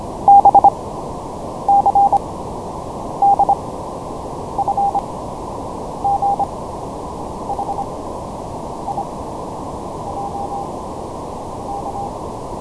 Morse code, heard as a keyed tone, about 25 wpm, 800 Hz, with noise. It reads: B C D F G H I J K
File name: signalNoise / MorseABC LPF1kHz